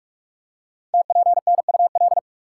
Morse code, heard as a keyed tone, about 44 wpm, 690 Hz. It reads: TPNUL
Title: Code Groups - 5 Letters - 45wpm